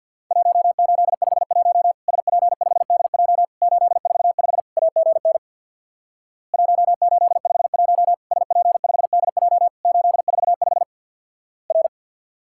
Morse code, heard as a keyed tone, about 49 wpm, 700 Hz. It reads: 1851 SP5DJ 845 AGN 1851 SP5DJ 845 R
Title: Worked All Europe DX Contest - QTC Exchange with Repeats - 50wpm